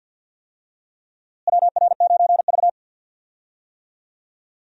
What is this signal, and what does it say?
Morse code, heard as a keyed tone, about 52 wpm, 700 Hz. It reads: WR9V
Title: USA Call Signs - 4 Characters - Repeated 3x - 50wpm